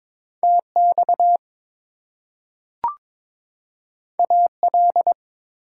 Morse code, heard as a keyed tone, about 22 wpm, 700 Hz. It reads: TX AL
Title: US States and Canadian Provinces 22wpm